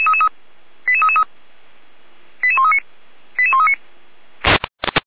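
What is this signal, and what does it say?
Five-tone selective call (ZVEI), heard as quick runs of five tones, tones of about 70 ms. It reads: 88333 88139 88139
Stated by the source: ZVEI five-tone selective call 88402, received on SDR, archived by Feuerwehr Orlamünde.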